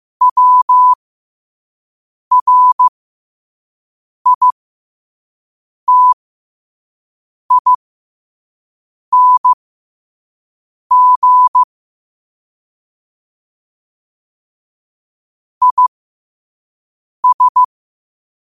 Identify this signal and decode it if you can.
Morse code, heard as a keyed tone, about 15 wpm, spaced as for 5 wpm, 1000 Hz. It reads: WRITING IS
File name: qotdfiles / 5wpm